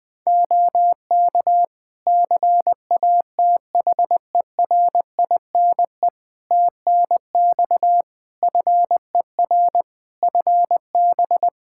Morse code, heard as a keyed tone, about 21 wpm, 700 Hz. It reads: OK CATHERINE TNX FER FB
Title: Practice QSO - Exchange 4 - Ending - 20wpm